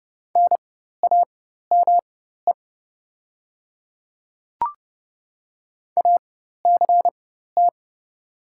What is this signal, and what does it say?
Morse code, heard as a keyed tone, about 30 wpm, spaced as for 14 wpm, 700 Hz. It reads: NAME ACT
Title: 4x Character Spacing - Top 200 Words - 30wpm